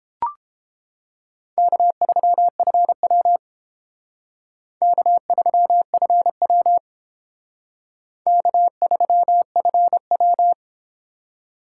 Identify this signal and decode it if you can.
Morse code, heard as a keyed tone, about 30 wpm, 690 Hz. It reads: K3FW K3FW K3FW